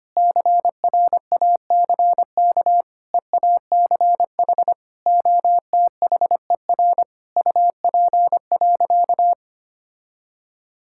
Morse code, heard as a keyed tone, about 25 wpm, 700 Hz. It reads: CRACK EACH OTHER UP.
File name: practice / 25wpm / All-Age Appropriate Jokes - 25wpm